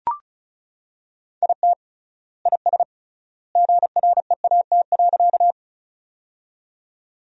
Morse code, heard as a keyed tone, about 34 wpm, 700 Hz. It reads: IT IS GREAT.
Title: 3x Word-Spacing - Sentences from Top 200 Words - 35wpm